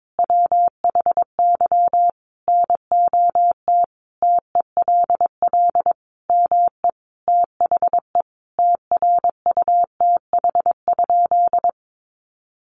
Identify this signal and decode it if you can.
Morse code, heard as a keyed tone, about 22 wpm, 700 Hz. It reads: WHY NOT TELL ME THE TRUTH?
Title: Sentences from Top 800 Words 22wpm